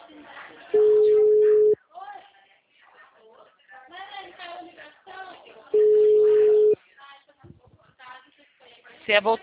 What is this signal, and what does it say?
Ringback tone (Europe, 425 Hz), with speech.